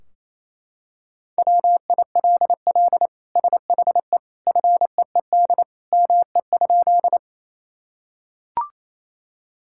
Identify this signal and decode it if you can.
Morse code, heard as a keyed tone, about 28 wpm, 700 Hz. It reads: WILL SHE FEED ME?
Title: Sentences from Top 1600 Words 28wpm